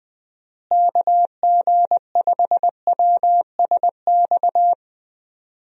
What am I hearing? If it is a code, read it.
Morse code, heard as a keyed tone, about 20 wpm, 700 Hz. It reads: KG5WSX
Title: USA Call Signs - 6 Characters - Repeated 3x - 20wpm